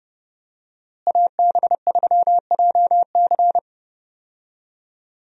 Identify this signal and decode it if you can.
Morse code, heard as a keyed tone, about 30 wpm, 700 Hz. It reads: AB3JC
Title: USA Call Signs - 5 Characters - Repeated 2x - 30wpm